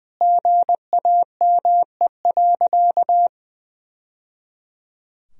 Morse code, heard as a keyed tone, about 20 wpm, 700 Hz. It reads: GAME.
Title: Sentences from Top 1000 Words - Review Part 4 - 20wpm